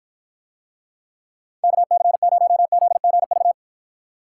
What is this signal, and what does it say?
Morse code, heard as a keyed tone, about 53 wpm, 700 Hz. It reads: KC0ZGV